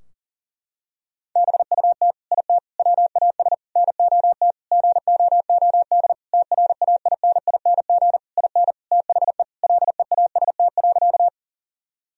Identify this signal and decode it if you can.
Morse code, heard as a keyed tone, about 40 wpm, 700 Hz. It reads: BUT IT WAS NOT GOOD TRAINING IN THE LEAST.
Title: Sentences from Top 1000 Words - Review Part 4 - 40wpm